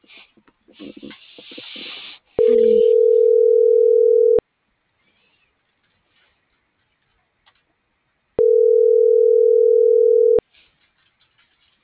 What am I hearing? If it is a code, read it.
Ringback tone (North America), 440 + 480 Hz.